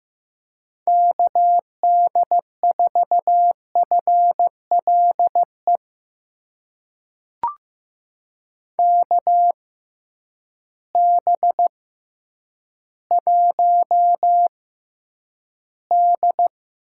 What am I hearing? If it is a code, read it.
Morse code, heard as a keyed tone, about 15 wpm, spaced as for 9 wpm, 700 Hz. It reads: KD4FLE KB1D
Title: USA Call Signs - 6x Character Spacing - 15wpm